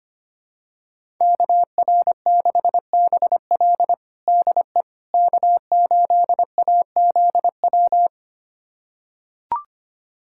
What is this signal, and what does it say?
Morse code, heard as a keyed tone, about 25 wpm, 700 Hz. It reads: KR6BL DE K8AZW
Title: USA Callsign DE Callsign - 5 Characters - 25wpm